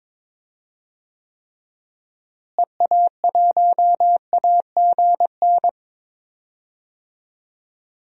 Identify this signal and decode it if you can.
Morse code, heard as a keyed tone, about 22 wpm, 700 Hz. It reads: EA1AGN
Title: DXCC-100 Call Signs - 6 Characters - Repeated 3x - 22wpm